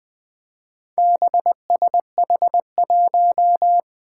Morse code, heard as a keyed tone, about 20 wpm, 700 Hz. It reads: BSH1
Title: Four Challenging Letter-Number Combinations 20wpm